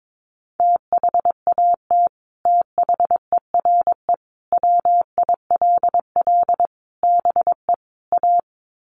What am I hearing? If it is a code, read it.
Morse code, heard as a keyed tone, about 22 wpm, 700 Hz. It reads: THAT THERE WILL BE A
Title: Sets of 5 Words Encore2 22wpm